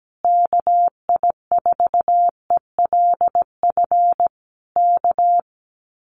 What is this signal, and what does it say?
Morse code, heard as a keyed tone, about 17 wpm, 700 Hz. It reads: KI4ELF K
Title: Common QSO Phrases Encore 17wpm